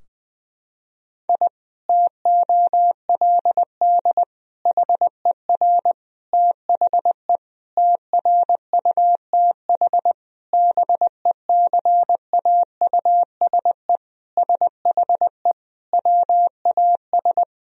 Morse code, heard as a keyed tone, about 20 wpm, 700 Hz. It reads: I TOLD HER THE TRUTH BECAUSE SHE WAS
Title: Sentences from Top 1000 Words - Review Part 3 - 20wpm